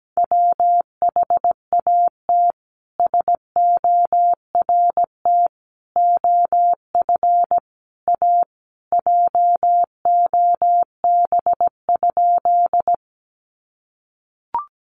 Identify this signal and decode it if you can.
Morse code, heard as a keyed tone, about 17 wpm, 700 Hz. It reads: WHAT SORT OF A JOB?
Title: Sentences from Top 1300 Words 17wpm